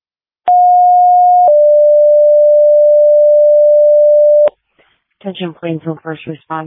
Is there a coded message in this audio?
Yes: two-tone page from 0.5 to 4.5 s, heard as a one-second tone, then a three-second tone, about 720 then 590 Hz, with speech.